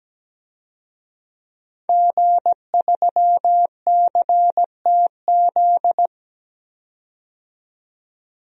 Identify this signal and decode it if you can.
Morse code, heard as a keyed tone, about 17 wpm, 700 Hz. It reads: G3CTZ
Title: DXCC-100 Call Signs - 5 Characters - 17wpm